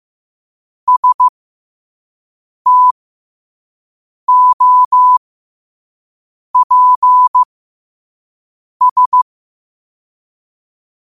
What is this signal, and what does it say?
Morse code, heard as a keyed tone, about 15 wpm, spaced as for 5 wpm, 1000 Hz. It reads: STOPS
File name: qotdfiles / 5wpm